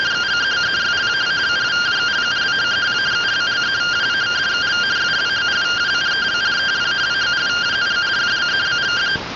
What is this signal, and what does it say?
Two-tone teleprinter signal, about 170 Hz shift.